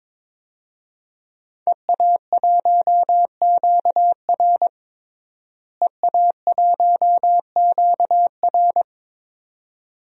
Morse code, heard as a keyed tone, about 22 wpm, 700 Hz. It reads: EA1QR EA1QR
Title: DXCC-100 Call Signs - 5 Characters - Repeated 2x - 22wpm